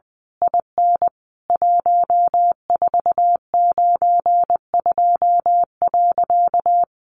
Morse code, heard as a keyed tone, about 20 wpm, 700 Hz. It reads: IN 1492.
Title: A Sound Of Thunder 20wpm